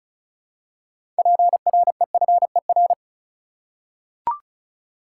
Morse code, heard as a keyed tone, about 34 wpm, 700 Hz. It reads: PREFER
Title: Six Letter Words - 35wpm